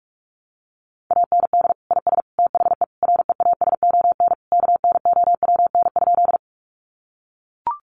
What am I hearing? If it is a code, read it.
Morse code, heard as a keyed tone, about 44 wpm, 700 Hz. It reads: AND IS THE REASON KNOWN?
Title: Sentences from Top 700 Words 45wpm